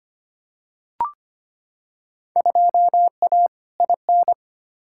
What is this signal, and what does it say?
Morse code, heard as a keyed tone, about 25 wpm, 700 Hz. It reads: 2A IN
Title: Field Day Exchanges - 25wpm